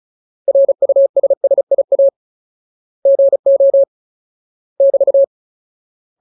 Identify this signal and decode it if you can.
Morse code, heard as a keyed tone, about 34 wpm, 550 Hz. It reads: RUSSIA GO <BT>
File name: Ultimate Word List / 35WPM Ultimate